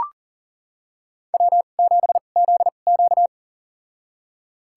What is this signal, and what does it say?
Morse code, heard as a keyed tone, about 40 wpm, spaced as for 29 wpm, 700 Hz. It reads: W7ZQ
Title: USA Call Signs - 4 Characters - 2x Character Spacing - 40wpm